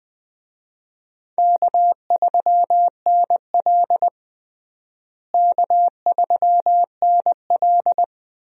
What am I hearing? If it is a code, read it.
Morse code, heard as a keyed tone, about 21 wpm, 700 Hz. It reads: K3NL K3NL
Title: USA Call Signs - 4 Characters - Repeated 3x - 20wpm